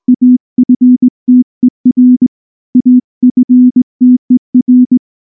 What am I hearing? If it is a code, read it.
Morse code, heard as a keyed tone, about 19 wpm, 250 Hz. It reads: AFTER AFTER
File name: morse / morse-training-20WPM-6